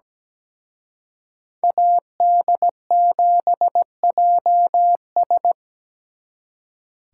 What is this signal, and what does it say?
Morse code, heard as a keyed tone, about 20 wpm, 700 Hz. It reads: AD7JS
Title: Speed Racing - USA 2x2 Call Signs 17wpm